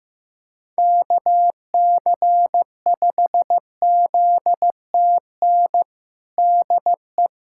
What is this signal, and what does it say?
Morse code, heard as a keyed tone, about 15 wpm, 700 Hz. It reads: KC5ZTN DE
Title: USA Callsign DE Callsign 15wpm